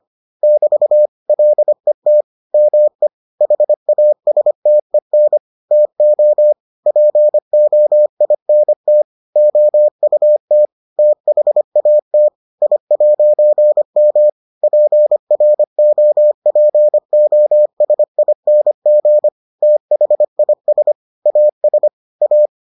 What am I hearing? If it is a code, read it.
Morse code, heard as a keyed tone, about 25 wpm, 600 Hz. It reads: <BT> LET ME HASTEN TO POINT OUT THAT I'M PROPOSING THIS AS A